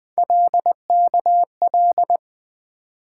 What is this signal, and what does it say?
Morse code, heard as a keyed tone, about 20 wpm, 700 Hz. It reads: LKL
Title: Three Letter-Number Combinations - 20wpm